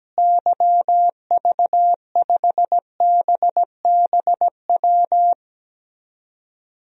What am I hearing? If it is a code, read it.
Morse code, heard as a keyed tone, about 17 wpm, 700 Hz. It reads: YV5BBW
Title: DXCC-100 Call Signs - 6 Characters - Repeated 3x - 17wpm